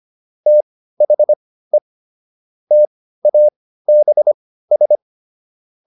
Morse code, heard as a keyed tone, about 25 wpm, spaced as for 15 wpm, 600 Hz. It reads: THE TABS